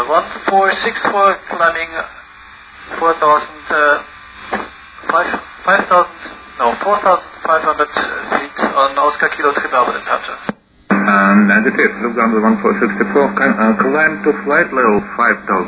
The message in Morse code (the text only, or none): none